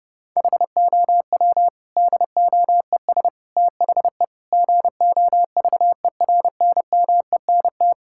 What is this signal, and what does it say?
Morse code, heard as a keyed tone, about 30 wpm, 700 Hz. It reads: HOW DOES THE GOVERNMENT
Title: Sentences from Top 1000 Words - Review Part 5 - 30wpm